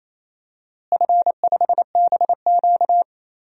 Morse code, heard as a keyed tone, about 28 wpm, 700 Hz. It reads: F5BQ
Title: DXCC-100 Call Signs - 4 Characters - Repeated 2x - 28wpm